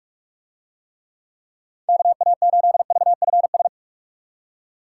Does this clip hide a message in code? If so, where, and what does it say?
Yes: Morse code from 1.9 to 3.7 s, heard as a keyed tone, about 47 wpm, 700 Hz. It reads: KA8VFS